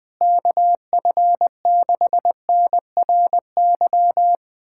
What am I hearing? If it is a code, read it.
Morse code, heard as a keyed tone, about 21 wpm, 700 Hz. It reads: KF6NRY
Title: USA Callsign DE Callsign 20wpm